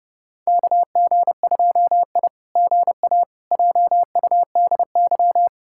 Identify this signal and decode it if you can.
Morse code, heard as a keyed tone, about 30 wpm, 700 Hz. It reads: KG2I GA JUDY
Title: SOTA QSO Practice 30wpm